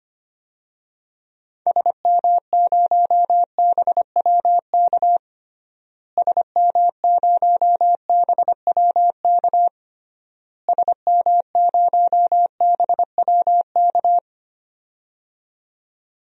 Morse code, heard as a keyed tone, about 25 wpm, 700 Hz. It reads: SM0BWK SM0BWK SM0BWK
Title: DXCC-100 Call Signs - 6 Characters - Repeated 3x - 25wpm